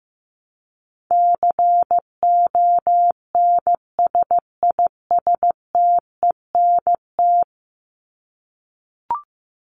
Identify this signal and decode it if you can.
Morse code, heard as a keyed tone, about 15 wpm, 700 Hz. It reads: CONSISTENT
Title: Top 1251-1500 Words 15wpm